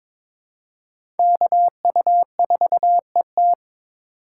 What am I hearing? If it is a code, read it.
Morse code, heard as a keyed tone, about 22 wpm, 700 Hz. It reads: KU4ET
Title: USA Call Signs - 5 Characters - 3x Character Spacing - 22wpm